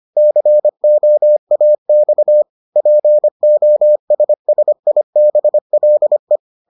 Morse code, heard as a keyed tone, about 25 wpm, 600 Hz. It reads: COAX POSSIBLE